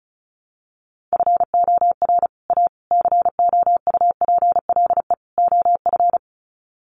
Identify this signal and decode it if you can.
Morse code, heard as a keyed tone, about 34 wpm, 700 Hz. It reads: FOR A COUPLE OF
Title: Sets of 4 Words 35wpm